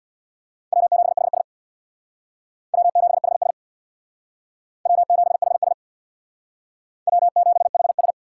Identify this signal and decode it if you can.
Morse code, heard as a keyed tone, about 61 wpm, 700 Hz. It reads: W7HS W7HS W7HS W7HS
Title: Speed Racing - USA 1x2 Call Signs 50wpm